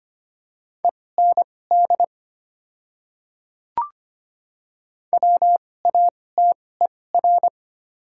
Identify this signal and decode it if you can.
Morse code, heard as a keyed tone, about 25 wpm, spaced as for 18 wpm, 690 Hz. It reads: END WATER
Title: 2x Character Spacing - Top 300 Words - 25wpm